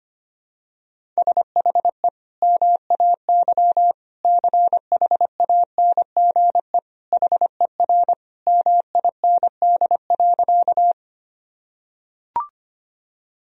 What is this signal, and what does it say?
Morse code, heard as a keyed tone, about 25 wpm, 700 Hz. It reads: SHE MAY CHANGE HER MIND.
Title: Sentences from Top 1000 Words - Review Part 1 - 25wpm